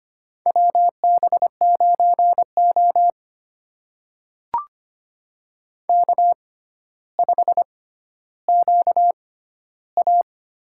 Morse code, heard as a keyed tone, about 25 wpm, spaced as for 13 wpm, 700 Hz. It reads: WB9O K5QA
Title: USA Call Signs - 4 Characters - 6x Character Spacing - 25wpm